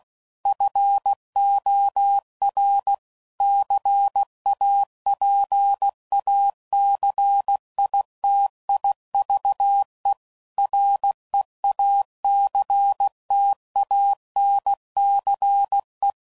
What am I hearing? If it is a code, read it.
Morse code, heard as a keyed tone, about 16 wpm, 800 Hz. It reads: FOR CAPACITIVE REACTANCE
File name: CW Code Groups @15WPM / Technical Subjects / XC